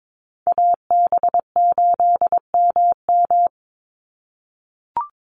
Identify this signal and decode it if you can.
Morse code, heard as a keyed tone, about 22 wpm, 700 Hz. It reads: AB8MM
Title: USA Call Signs 22wpm